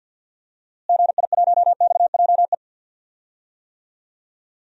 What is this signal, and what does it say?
Morse code, heard as a keyed tone, about 50 wpm, 700 Hz. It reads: GI1XJE
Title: DXCC-100 Call Signs - 6 Characters - Repeated 2x - 50wpm